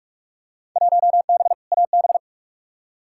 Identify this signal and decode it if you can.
Morse code, heard as a keyed tone, about 45 wpm, 700 Hz. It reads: 1B AB
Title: Field Day Exchanges - 45wpm